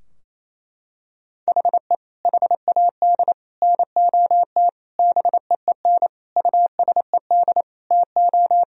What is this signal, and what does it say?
Morse code, heard as a keyed tone, about 28 wpm, 700 Hz. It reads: HE HAD NOT BEEN USED TO
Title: Sentences from Top 1000 Words - Review Part 4 - 28wpm